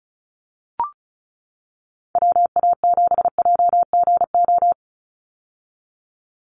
Morse code, heard as a keyed tone, about 34 wpm, 700 Hz. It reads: WA7JGO
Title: USA Call Signs 35wpm